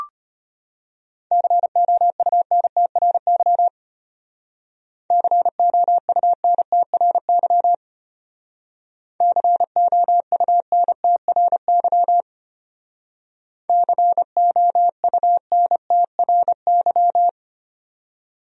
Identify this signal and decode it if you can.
Morse code, heard as a keyed tone, about 32 wpm, 700 Hz. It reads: COUNTRY COUNTRY COUNTRY COUNTRY